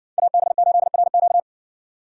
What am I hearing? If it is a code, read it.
Morse code, heard as a keyed tone, about 60 wpm, 700 Hz. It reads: AB8RQ